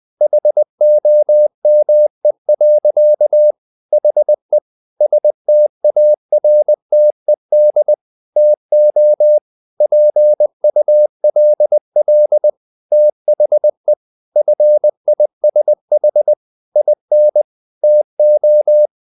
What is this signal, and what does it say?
Morse code, heard as a keyed tone, about 20 wpm, 600 Hz. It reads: HOME. HE STARTED TO PULL THE FISH IN TO
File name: the old man and the sea part 2 20wpm